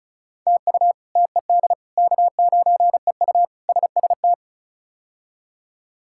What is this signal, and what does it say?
Morse code, heard as a keyed tone, about 36 wpm, 700 Hz. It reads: TU TED K9EU SST
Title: K1USN SST QSO Practice - 35wpm